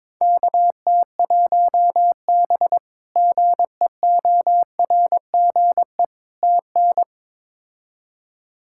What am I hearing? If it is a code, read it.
Morse code, heard as a keyed tone, about 22 wpm, 700 Hz. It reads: KT1B GEORGE TN
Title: K1USN SST QSO Practice - 22wpm